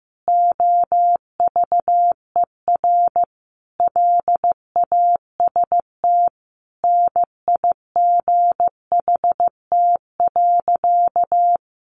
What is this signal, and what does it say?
Morse code, heard as a keyed tone, about 15 wpm, 700 Hz. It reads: OVER LAST NIGHT.
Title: Sentences from Top 300 Words 15wpm